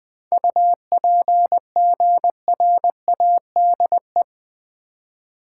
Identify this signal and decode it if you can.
Morse code, heard as a keyed tone, about 20 wpm, 700 Hz. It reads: UPGRADE
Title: 7 Letter Words - First 3 Letters Given - 20wpm